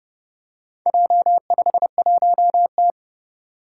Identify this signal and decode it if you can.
Morse code, heard as a keyed tone, about 30 wpm, 700 Hz. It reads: J51T
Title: Four Challenging Letter-Number Combinations 30wpm